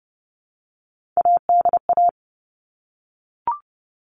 Morse code, heard as a keyed tone, about 30 wpm, 700 Hz. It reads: ADA
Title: All Cut Numbers - 3-Digit 30wpm